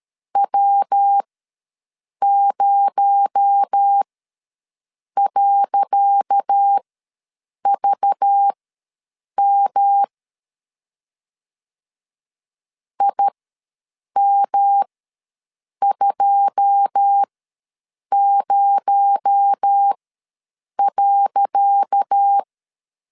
Morse code, heard as a keyed tone, about 12 wpm, spaced as for 7 wpm, 800 Hz. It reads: W0.VM IM20.
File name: LEZIONI / 10 SIGNS / 10C DOT ws 12wpm 3548